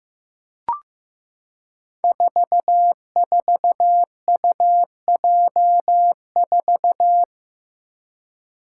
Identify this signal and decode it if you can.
Morse code, heard as a keyed tone, about 16 wpm, 700 Hz. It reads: 44UJ4
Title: Five Challenging Letter-Number Combinations 15wpm